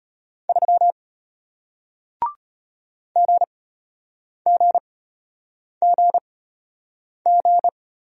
Morse code, heard as a keyed tone, about 34 wpm, spaced as for 8 wpm, 700 Hz. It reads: 3 GGGG